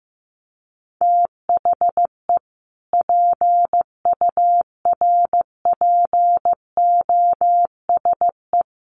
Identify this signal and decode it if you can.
Morse code, heard as a keyed tone, about 15 wpm, 700 Hz. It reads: THE PURPOSE
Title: Sets of 5 Words 15wpm